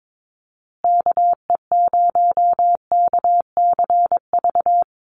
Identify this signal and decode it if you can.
Morse code, heard as a keyed tone, about 22 wpm, 700 Hz. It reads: KE0KCV
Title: USA Call Signs Repeated 3x 22wpm